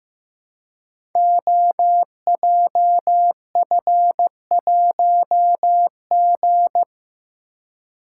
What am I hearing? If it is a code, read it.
Morse code, heard as a keyed tone, about 15 wpm, 700 Hz. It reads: OJF1G
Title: Code Groups - 5 Letter:Numbers - 15wpm